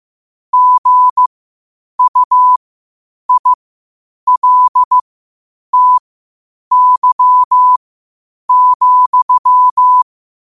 Morse code, heard as a keyed tone, about 15 wpm, spaced as for 8 wpm, 1000 Hz. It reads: GUILTY,
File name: qotdfiles / 8wpm